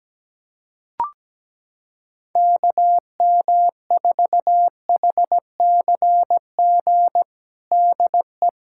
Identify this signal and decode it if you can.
Morse code, heard as a keyed tone, about 17 wpm, 700 Hz. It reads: KM4HCG DE
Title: USA Callsign DE Callsign 17wpm